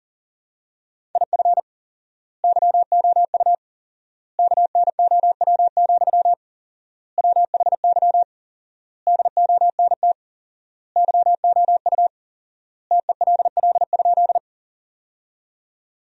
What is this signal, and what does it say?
Morse code, heard as a keyed tone, about 40 wpm, 700 Hz. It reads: IF YOU KNOW, WHY DONT YOU TELL?
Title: 4x Word-Spacing - Sentences from Top 300 Words - 40wpm